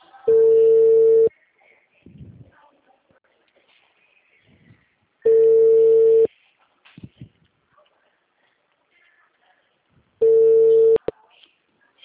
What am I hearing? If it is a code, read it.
Ringback tone (China, 450 Hz).
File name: out- -2021- - - .67983